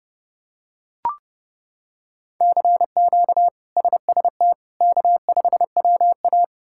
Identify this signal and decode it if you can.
Morse code, heard as a keyed tone, about 32 wpm, 700 Hz. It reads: CQ SST K5WA
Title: K1USN SST QSO Practice - 30wpm